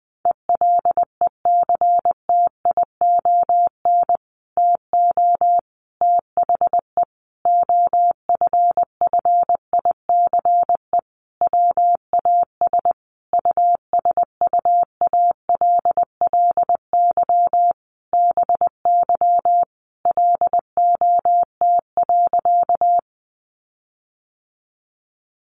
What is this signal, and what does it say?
Morse code, heard as a keyed tone, about 20 wpm, 700 Hz. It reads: ELECTION TO THE OFFICE WAS USUALLY BY LOT.
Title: Sentences from Top 1500 Words 20wpm